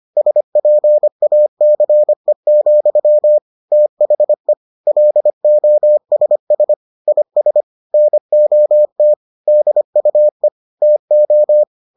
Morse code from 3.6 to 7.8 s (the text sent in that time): THE LOSS IS